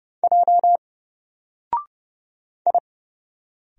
Morse code, heard as a keyed tone, about 30 wpm, 700 Hz. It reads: J I
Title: Single Letter-Number 30wpm